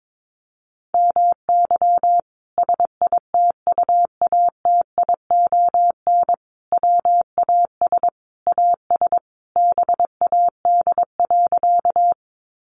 Morse code, heard as a keyed tone, about 22 wpm, 700 Hz. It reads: MY SITUATION WAS AS BAD.
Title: Sentences from Top 1200 Words 22wpm